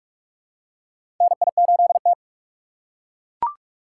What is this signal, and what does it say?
Morse code, heard as a keyed tone, about 42 wpm, 700 Hz. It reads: NI8T
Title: USA Call Signs - 4 Characters - 6x Character Spacing - 45wpm